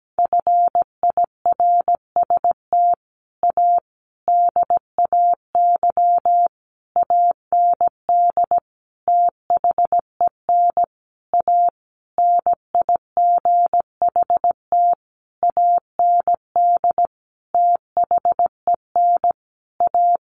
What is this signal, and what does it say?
Morse code, heard as a keyed tone, about 17 wpm, 700 Hz. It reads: FIRST A DAY AND THEN A NIGHT AND THEN A
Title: A Sound Of Thunder 17wpm